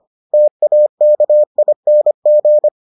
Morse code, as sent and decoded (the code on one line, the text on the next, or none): - .- -.- .. -. --.
TAKING